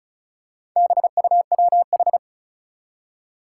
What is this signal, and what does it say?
Morse code, heard as a keyed tone, about 36 wpm, 700 Hz. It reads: BUWH
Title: Four Challenging Letter-Number Combinations 35wpm